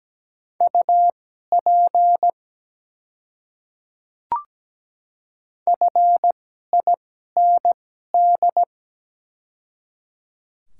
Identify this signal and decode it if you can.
Morse code, heard as a keyed tone, about 17 wpm, spaced as for 12 wpm, 700 Hz. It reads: UP FIND
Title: 2x Character Spacing - Top 100 Words - 17wpm